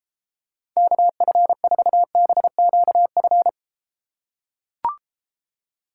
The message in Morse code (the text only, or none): KF4BQF